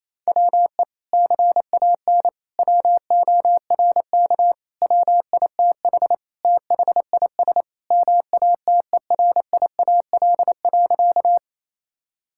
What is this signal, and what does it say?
Morse code, heard as a keyed tone, about 27 wpm, 700 Hz. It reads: WE CAN WORK WITH THIS MATERIAL.
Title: Sentences from Top 1000 Words - Review Part 5 - 28wpm